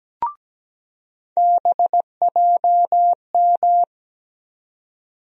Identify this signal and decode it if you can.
Morse code, heard as a keyed tone, about 17 wpm, 700 Hz. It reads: BJM
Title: USA Call Sign Postfixes - 3 Letters 17wpm